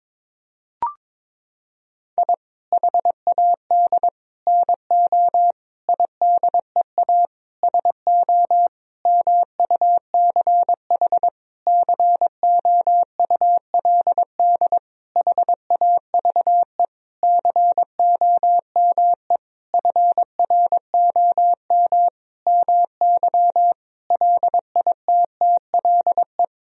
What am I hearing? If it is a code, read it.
Morse code, heard as a keyed tone, about 22 wpm, 700 Hz. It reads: I HAD NO IDEA SO MUCH COULD HAVE COME FROM MY LITTLE
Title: Sentences from Top 1000 Words - Review Part 4 - 22wpm